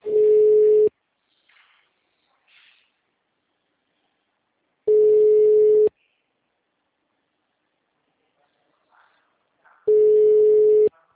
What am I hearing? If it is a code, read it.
Ringback tone (Europe, 425 Hz).
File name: out- -2007- - - .47750